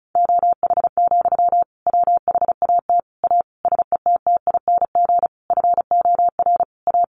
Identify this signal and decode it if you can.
Morse code, heard as a keyed tone, about 34 wpm, 700 Hz. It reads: OH, WHAT A SETTING FOR A
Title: Sentences from Top 1300 Words 35wpm